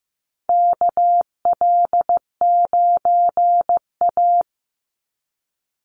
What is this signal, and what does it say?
Morse code, heard as a keyed tone, about 15 wpm, 700 Hz. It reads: KL9A
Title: Call Signs Repeated 3x - 15wpm